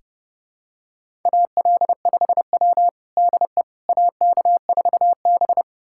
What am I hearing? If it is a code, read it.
Morse code, heard as a keyed tone, about 30 wpm, 700 Hz. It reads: AL5W DE AK4B
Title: USA Callsign DE Callsign - 4 Characters - 30wpm